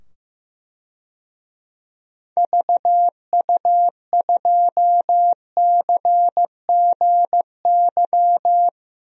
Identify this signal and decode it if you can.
Morse code, heard as a keyed tone, about 15 wpm, 700 Hz. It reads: VU2CGY
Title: DXCC-100 Call Signs - 6 Characters - 15wpm